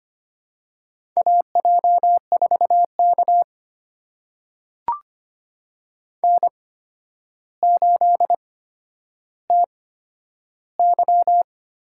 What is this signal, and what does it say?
Morse code, heard as a keyed tone, about 25 wpm, spaced as for 11 wpm, 700 Hz. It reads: AJ4K N8TY
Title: USA Call Signs - 8x Character Spacing - 25wpm